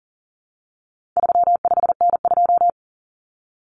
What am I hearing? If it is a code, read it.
Morse code, heard as a keyed tone, about 40 wpm, 700 Hz. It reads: 35N2
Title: Common Cut Numbers - 4-Digit 40wpm